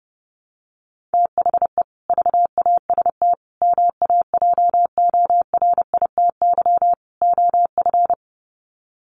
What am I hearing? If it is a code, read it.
Morse code, heard as a keyed tone, about 30 wpm, 700 Hz. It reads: THE VAST MAJORITY OF
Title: Sets of 4 Words 30wpm